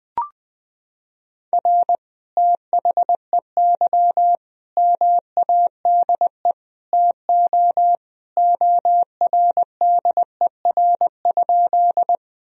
Morse code, heard as a keyed tone, about 21 wpm, 700 Hz. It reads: R THEY MADE TO ORDER?
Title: Sentences from Top 400 Words and Abbreviations 20wpm